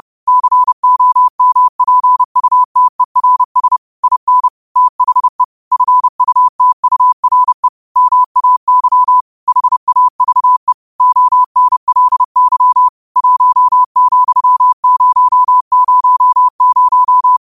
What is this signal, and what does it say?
Morse code, heard as a keyed tone, about 30 wpm, 1000 Hz. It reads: COMPUTERS IN THE FUTURE MAY HAVE ONLY 1,000
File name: qotdfiles / 30wpm